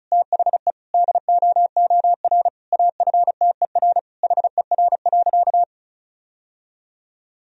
Morse code, heard as a keyed tone, about 36 wpm, 700 Hz. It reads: THE DOOR AFTER HER.
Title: Sentences from Top 1000 Words - Review Part 1 - 35wpm